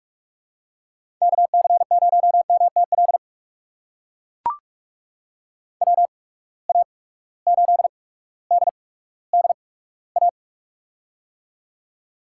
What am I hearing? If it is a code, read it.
Morse code, heard as a keyed tone, about 44 wpm, spaced as for 19 wpm, 700 Hz. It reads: KC0MTL WA8DDA
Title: USA Call Signs - 6 Characters - 8x Character Spacing - 45wpm